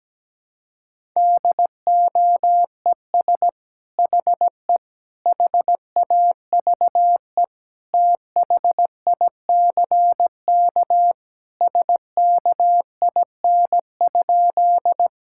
Morse code, heard as a keyed tone, about 17 wpm, 700 Hz. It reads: DOES HE HAVE THICK SKIN?
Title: Sentences from Top 1600 Words 17wpm